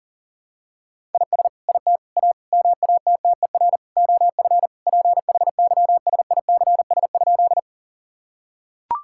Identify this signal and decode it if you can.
Morse code, heard as a keyed tone, about 40 wpm, 700 Hz. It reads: IS IT A MATTER OF PHYSICS?
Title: Sentences from Top 1000 Words - Review Part 5 - 40wpm